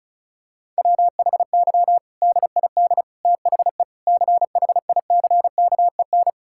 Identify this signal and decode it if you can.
Morse code, heard as a keyed tone, about 36 wpm, 700 Hz. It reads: WHY DID THE CHICKEN
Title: Sentences from Top 1000 Words - Review Part 5 - 35wpm